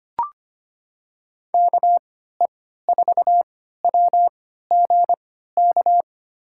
Morse code, heard as a keyed tone, about 25 wpm, spaced as for 14 wpm, 700 Hz. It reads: KE4WGK
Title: USA Call Signs - 3x Character Spacing - 25wpm